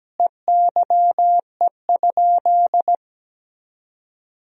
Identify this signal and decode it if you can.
Morse code, heard as a keyed tone, about 17 wpm, 700 Hz. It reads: EYE?
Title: Sentences from Top 300 Words and Abbreviations 17wpm